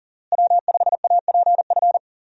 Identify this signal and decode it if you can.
Morse code, heard as a keyed tone, about 40 wpm, 700 Hz. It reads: W5APF
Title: USA Call Signs - 5 Characters - Repeated 3x - 40wpm